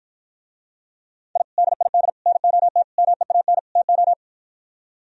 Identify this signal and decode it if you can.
Morse code, heard as a keyed tone, about 53 wpm, 700 Hz. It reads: I DID NOT MEAN TO